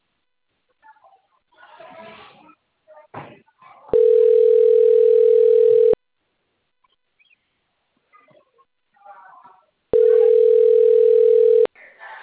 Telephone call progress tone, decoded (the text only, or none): ringback tone